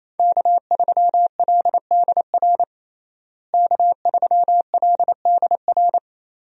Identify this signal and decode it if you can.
Morse code, heard as a keyed tone, about 28 wpm, 700 Hz. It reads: K3LDR K3LDR
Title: USA Call Signs - 5 Characters - Repeated 3x - 28wpm